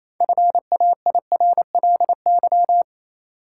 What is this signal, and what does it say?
Morse code, heard as a keyed tone, about 28 wpm, 700 Hz. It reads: FAIRLY
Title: Sentences from Top 1100 Words - New Word Drill - 28wpm